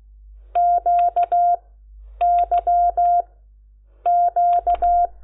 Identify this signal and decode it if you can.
Morse code, heard as a keyed tone, about 16 wpm, spaced as for 8 wpm, 700 Hz. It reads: QYQ
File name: Lecciones / CLASE Nº10